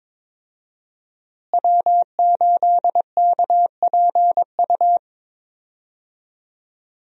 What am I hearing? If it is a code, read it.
Morse code, heard as a keyed tone, about 22 wpm, 700 Hz. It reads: W8KPU
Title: USA Call Signs - 5 Characters - Repeated 2x - 22wpm